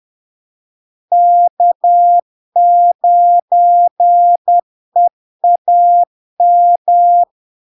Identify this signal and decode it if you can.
Morse code, heard as a keyed tone, about 10 wpm, 700 Hz. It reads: K9EAM
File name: LARK-FD-2022-Log-in-CW-10wpm-700-Hz